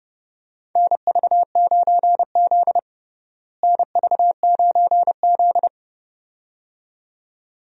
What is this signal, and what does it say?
Morse code, heard as a keyed tone, about 30 wpm, 700 Hz. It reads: NV9Z NV9Z
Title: USA Call Signs - 4 Characters - Repeated 2x - 30wpm